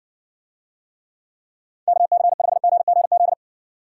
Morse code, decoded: KC5GGZ

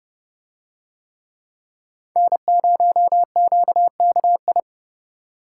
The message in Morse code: N0QKI

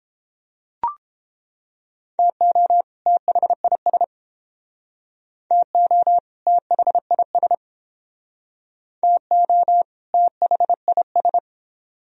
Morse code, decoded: TO THIS TO THIS TO THIS

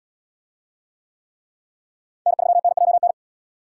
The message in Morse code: N3NJN

